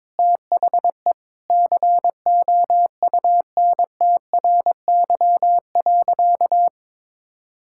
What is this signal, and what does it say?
Morse code, heard as a keyed tone, about 22 wpm, 700 Hz. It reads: THE COUNTRY.
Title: Sentences from Top 1000 Words - Review Part 6 - 22wpm